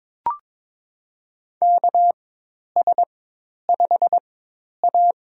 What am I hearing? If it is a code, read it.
Morse code, heard as a keyed tone, about 22 wpm, spaced as for 10 wpm, 700 Hz. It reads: KS5A
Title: USA Call Signs - 4 Characters - 4x Character Spacing - 22wpm